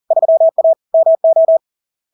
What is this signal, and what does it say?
Morse code, heard as a keyed tone, about 40 wpm, 650 Hz. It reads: 3A MO
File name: LARK-FD-2022-Log-in-CW-40wpm-650-Hz